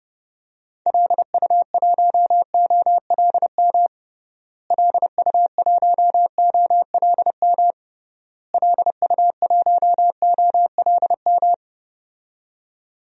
Morse code, heard as a keyed tone, about 30 wpm, 700 Hz. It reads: LU1OLM LU1OLM LU1OLM
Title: DXCC-100 Call Signs - 6 Characters - Repeated 3x - 30wpm